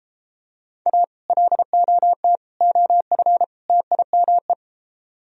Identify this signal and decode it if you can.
Morse code, heard as a keyed tone, about 33 wpm, 700 Hz. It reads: A LOT OF TIME